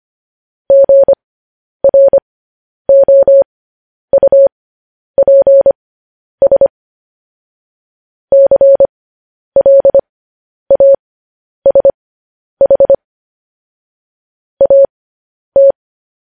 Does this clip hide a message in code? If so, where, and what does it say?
Yes: Morse code from 0.7 to 15.7 s, heard as a keyed tone, about 25 wpm, spaced as for 10 wpm, 550 Hz. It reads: GROUPS CLASH AT